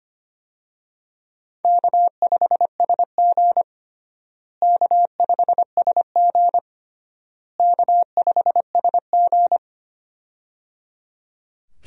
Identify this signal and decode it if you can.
Morse code, heard as a keyed tone, about 25 wpm, 700 Hz. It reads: K5SG K5SG K5SG
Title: USA Call Signs - 4 Characters - Repeated 3x - 25wpm